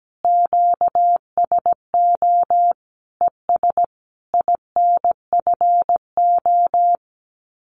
Morse code, heard as a keyed tone, about 17 wpm, 700 Hz. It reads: QSO ES INFO
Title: Common QSO Phrases Encore 17wpm